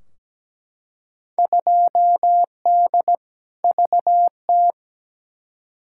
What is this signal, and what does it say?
Morse code, heard as a keyed tone, about 17 wpm, 700 Hz. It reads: 2D VT
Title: Field Day Exchanges Encore - 17wpm